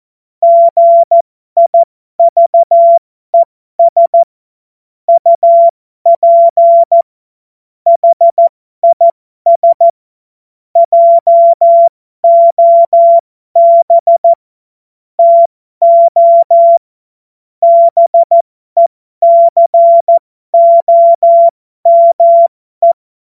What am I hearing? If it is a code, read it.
Morse code, heard as a keyed tone, about 14 wpm, spaced as for 12 wpm, 680 Hz. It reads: GIVES UP HIS JOB TO BECOME
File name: Morsekurs / Witze